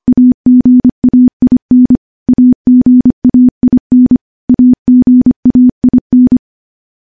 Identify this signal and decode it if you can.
Morse code, heard as a keyed tone, about 25 wpm, 250 Hz. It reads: AGAIN AGAIN AGAIN
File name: morse / morse-training-25WPM